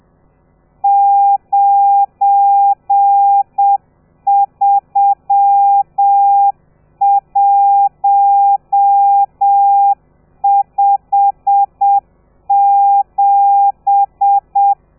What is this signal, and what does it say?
Morse code, heard as a keyed tone, about 7 wpm, 800 Hz. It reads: 93157
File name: telegrafia / i2tfj / Lez12 30